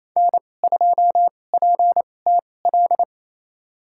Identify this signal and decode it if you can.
Morse code, heard as a keyed tone, about 28 wpm, spaced as for 20 wpm, 700 Hz. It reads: N2PTL
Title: USA Call Signs - 5 Characters - 2x Character Spacing - 28wpm